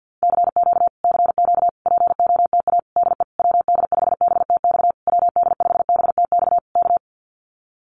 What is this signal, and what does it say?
Morse code, heard as a keyed tone, about 50 wpm, 700 Hz. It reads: CQ CQ POTA DE WD5BTX WD5BTX K
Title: Common QSO Phrases Encore 50wpm